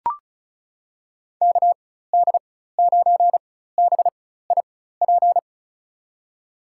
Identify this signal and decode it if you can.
Morse code, heard as a keyed tone, about 36 wpm, spaced as for 16 wpm, 700 Hz. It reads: KD9BIP
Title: USA Call Signs - 6 Characters - 4x Character Spacing - 35wpm